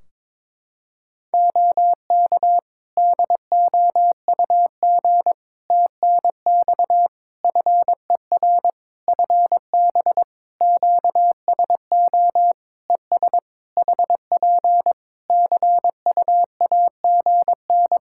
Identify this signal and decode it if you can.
Morse code, heard as a keyed tone, about 21 wpm, 700 Hz. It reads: OK DOUG TNX FER FB QSO ES HP CUAGN
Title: K1USN SST QSO Practice 22wpm